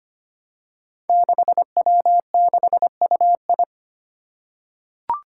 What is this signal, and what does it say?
Morse code, heard as a keyed tone, about 25 wpm, 700 Hz. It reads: 6W6UI
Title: Five Challenging Letter-Number Combinations 25wpm